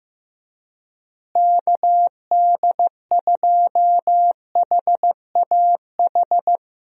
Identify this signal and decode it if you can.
Morse code, heard as a keyed tone, about 15 wpm, 700 Hz. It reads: KD2HAH
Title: USA Call Signs - 6 Characters - Repeated 3x - 15wpm